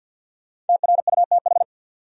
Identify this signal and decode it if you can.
Morse code, heard as a keyed tone, about 52 wpm, 700 Hz. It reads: TRUTH
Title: Sentences from Top 800 Words - New Word Drill - 50wpm